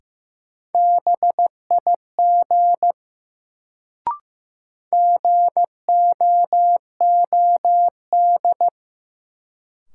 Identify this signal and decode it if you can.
Morse code, heard as a keyed tone, about 15 wpm, 700 Hz. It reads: BIG GOOD
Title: Sentences from Top 200 Words - New Word Drill - 15wpm